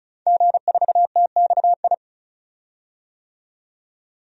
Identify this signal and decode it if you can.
Morse code, heard as a keyed tone, about 36 wpm, 700 Hz. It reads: G4TXI
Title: DXCC-100 Call Signs - 5 Characters - 35wpm